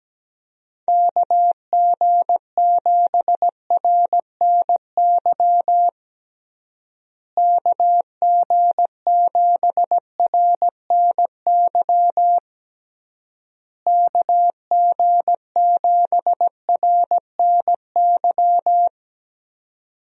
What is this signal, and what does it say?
Morse code, heard as a keyed tone, about 17 wpm, 700 Hz. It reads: KG7RNY KG7RNY KG7RNY
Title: USA Call Signs - 6 Characters - Repeated 3x - 17wpm